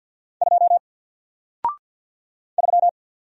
Morse code, heard as a keyed tone, about 48 wpm, 700 Hz. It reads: 2 3
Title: Single Letter-Number Alt 50wpm